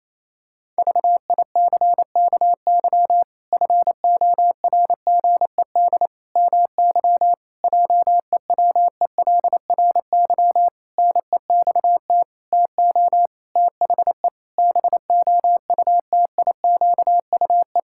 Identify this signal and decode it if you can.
Morse code, heard as a keyed tone, about 28 wpm, 700 Hz. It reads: VICKY FORGED MY JEWELRY NEXT TO THE BOUTIQUE
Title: Pangram Challenge v2 - 28wpm